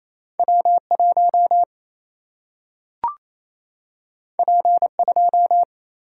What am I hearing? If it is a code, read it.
Morse code, heard as a keyed tone, about 28 wpm, 700 Hz. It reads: W1 P2
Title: Single-Letter Single-Number 28wpm